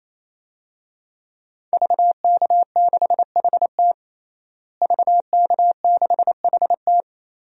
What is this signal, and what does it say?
Morse code, heard as a keyed tone, about 28 wpm, 700 Hz. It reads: VK6HT VK6HT
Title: DXCC-100 Call Signs - 5 Characters - Repeated 3x - 28wpm